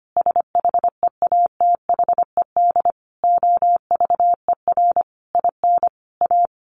Morse code, heard as a keyed tone, about 25 wpm, 700 Hz. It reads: SHEATHED OVER IN A
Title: A Sound Of Thunder 25wpm